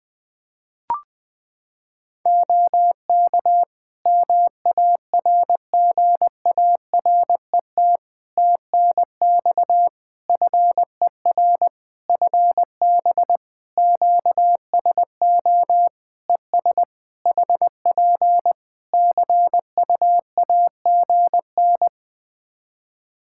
Morse code, heard as a keyed tone, about 21 wpm, 700 Hz. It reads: OK MARGARET TNX FER FB QSO ES HP CUAGN
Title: Practice QSO - Exchange 4 - Ending - 20wpm